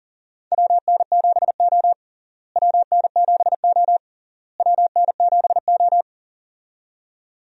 Morse code, heard as a keyed tone, about 37 wpm, 700 Hz. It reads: WN7O WN7O WN7O
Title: USA Call Signs - 4 Characters - Repeated 3x - 40wpm